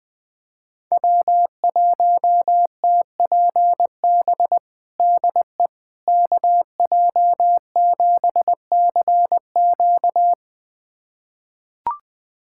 Morse code, heard as a keyed tone, about 19 wpm, 700 Hz. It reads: W1TPB DE KJ7CQ
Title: USA Callsign DE Callsign - 5 Characters - 20wpm